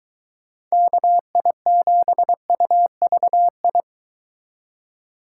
Morse code, heard as a keyed tone, about 23 wpm, 700 Hz. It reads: KI7UVI